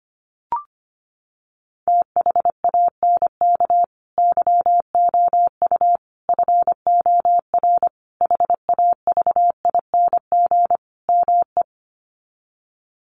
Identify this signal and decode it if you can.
Morse code, heard as a keyed tone, about 25 wpm, 700 Hz. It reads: THANK YOU FOR HAVING ME
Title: Sets of 5 Words Encore 25wpm